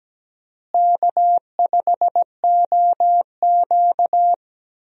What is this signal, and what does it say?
Morse code, heard as a keyed tone, about 17 wpm, 700 Hz. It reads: K5OQ
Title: USA Call Signs - 4 Characters - Repeated 2x - 17wpm